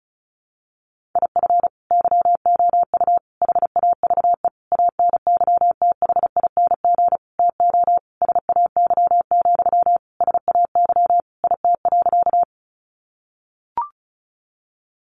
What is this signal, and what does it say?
Morse code, heard as a keyed tone, about 34 wpm, 700 Hz. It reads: IF YOU HAVE ANYTHING TO SAY, SAY IT.
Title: Sentences from Top 1000 Words 35wpm